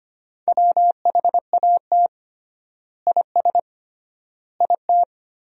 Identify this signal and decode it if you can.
Morse code, heard as a keyed tone, about 25 wpm, 700 Hz. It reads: WHAT IS IT
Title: 3x Word-Spacing - Sentences from Top 100 Words - 25wpm